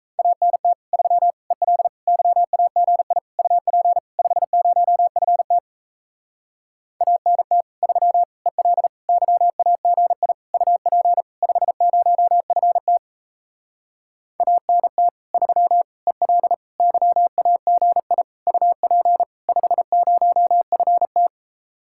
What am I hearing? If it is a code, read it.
Morse code, heard as a keyed tone, about 38 wpm, 700 Hz. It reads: ANT 3 EL YAGI UP 50FT ANT 3 EL YAGI UP 50FT ANT 3 EL YAGI UP 50FT